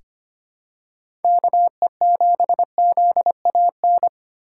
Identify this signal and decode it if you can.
Morse code, heard as a keyed tone, about 25 wpm, 700 Hz. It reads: KE7ZAN
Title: USA Call Signs - 6 Characters - 4x Character Spacing - 25wpm